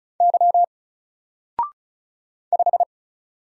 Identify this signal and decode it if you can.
Morse code, heard as a keyed tone, about 35 wpm, 700 Hz. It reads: Y 5
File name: Single-Letter-Number-35wpm